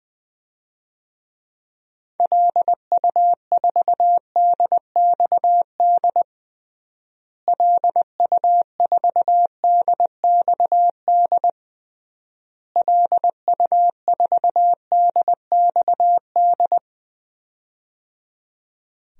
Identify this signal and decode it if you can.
Morse code, heard as a keyed tone, about 21 wpm, 700 Hz. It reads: LU4DXD LU4DXD LU4DXD
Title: DXCC-100 Call Signs - 6 Characters - Repeated 3x - 20wpm